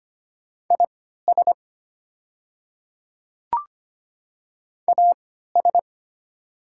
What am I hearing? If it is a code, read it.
Morse code, heard as a keyed tone, about 25 wpm, spaced as for 14 wpm, 700 Hz. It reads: IS AS
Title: 3x Character Spacing - Top 100 Words - 25wpm